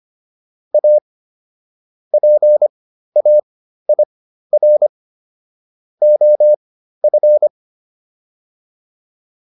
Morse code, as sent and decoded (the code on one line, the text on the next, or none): .- / .--. .- .. .-. / --- ..-.
A PAIR OF